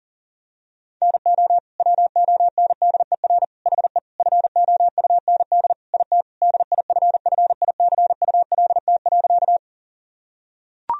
Morse code, heard as a keyed tone, about 40 wpm, 700 Hz. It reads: NO WONDER HE FOUND IT DIFFICULT.
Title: Sentences from Top 1000 Words - Review Part 3 - 40wpm